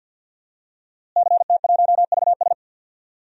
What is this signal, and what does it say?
Morse code, heard as a keyed tone, about 45 wpm, 700 Hz. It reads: CT1VS
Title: DXCC-100 Call Signs - 5 Characters - Repeated 2x - 50wpm